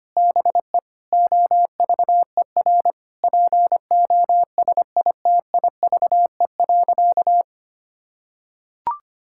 Morse code, heard as a keyed tone, about 25 wpm, 700 Hz. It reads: BE OVER POSITIVE.
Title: Sentences from Top 1000 Words - Review Part 5 - 25wpm